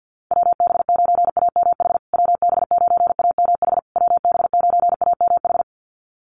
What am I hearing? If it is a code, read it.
Morse code, heard as a keyed tone, about 50 wpm, 700 Hz. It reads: WB9AMH WB9AMH WB9AMH
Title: USA Call Signs Repeated 3x 50wpm